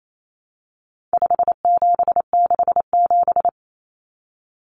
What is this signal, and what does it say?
Morse code, heard as a keyed tone, about 28 wpm, 700 Hz. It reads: 5767
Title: Common Cut Numbers - 4-Digit 28wpm